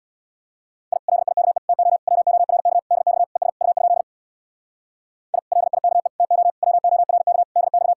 Morse code, heard as a keyed tone, about 59 wpm, 700 Hz. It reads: I LIKE TO WORK DX ES U? I LIKE TO WORK DX
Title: Speed-Racing Common QSO Phrases - 50wpm